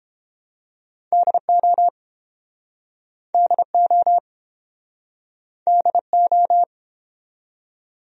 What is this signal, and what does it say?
Morse code, heard as a keyed tone, about 34 wpm, 700 Hz. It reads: DO DO DO